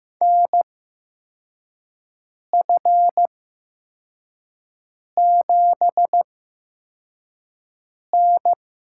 Morse code, heard as a keyed tone, about 15 wpm, spaced as for 4 wpm, 700 Hz. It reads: NF7N
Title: USA Call Signs - 4 Characters - 8x Character Spacing - 15wpm